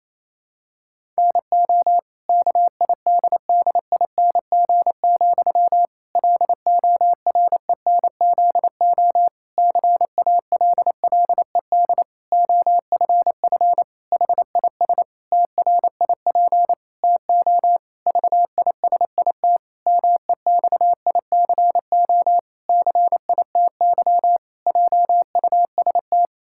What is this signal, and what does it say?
Morse code, heard as a keyed tone, about 27 wpm, 700 Hz. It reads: NO KIDDING, LORENZO CALLED OFF HIS TRIP TO VISIT MEXICO CITY JUST
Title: Pangram Challenge v2 - 28wpm